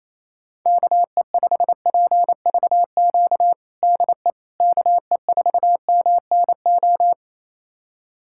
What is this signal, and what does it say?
Morse code, heard as a keyed tone, about 28 wpm, 700 Hz. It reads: KE5PVQ DE KE4MNO
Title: USA Callsign DE Callsign 28wpm